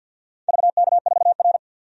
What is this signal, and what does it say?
Morse code, heard as a keyed tone, about 47 wpm, 700 Hz. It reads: VK4R
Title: DXCC-100 Call Signs - 4 Characters - 50wpm